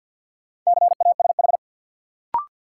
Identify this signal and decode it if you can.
Morse code, heard as a keyed tone, about 51 wpm, 700 Hz. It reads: CASH
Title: Sentences from Top 700 Words - New Word Drill - 50wpm